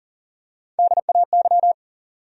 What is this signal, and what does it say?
Morse code, heard as a keyed tone, about 40 wpm, 700 Hz. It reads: DAY
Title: Top 100 Words Repeated 40wpm